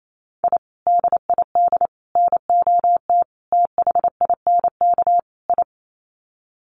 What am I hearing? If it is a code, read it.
Morse code, heard as a keyed tone, about 28 wpm, 700 Hz. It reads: I DID NOT THINK I
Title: Sets of 5 Words Encore 28wpm